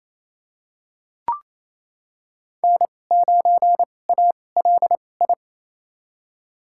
Morse code, heard as a keyed tone, about 27 wpm, spaced as for 20 wpm, 700 Hz. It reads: N9ALI
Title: USA Call Signs - 5 Characters - 2x Character Spacing - 28wpm